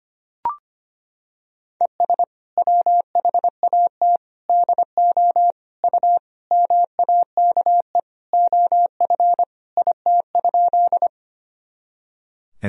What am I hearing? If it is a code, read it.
Morse code, heard as a keyed tone, about 25 wpm, 700 Hz. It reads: ES WHAT DO U MAKE OF IT?
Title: Sentences from Top 100 Words and Abbreviations 25wpm